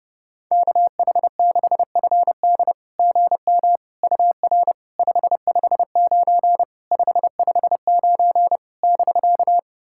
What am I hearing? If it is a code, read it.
Morse code, heard as a keyed tone, about 30 wpm, 690 Hz. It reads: KH6FD GM UR 559 559 <BK>
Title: SOTA QSO Practice 30wpm